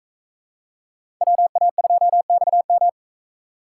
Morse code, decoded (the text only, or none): WA2XM